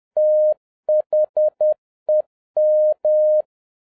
Morse code, heard as a keyed tone, about 10 wpm, 610 Hz. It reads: THEM
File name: 2011-07-mp3 / n46 JOHN K3TN 10wpm